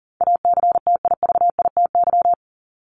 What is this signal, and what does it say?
Morse code, heard as a keyed tone, about 40 wpm, 700 Hz. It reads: ACTIVITY
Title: Top 501-750 Words 40wpm